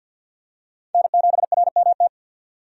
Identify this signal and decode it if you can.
Morse code, heard as a keyed tone, about 52 wpm, 700 Hz. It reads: N7RMT
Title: USA Call Signs - 5 Characters - 50wpm